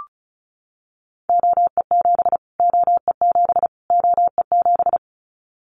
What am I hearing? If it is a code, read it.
Morse code, heard as a keyed tone, about 34 wpm, 700 Hz. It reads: OE7 OE7 OE7
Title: Callsign Prefixes for Top 25 DXCC - Repeated 3x - 35wpm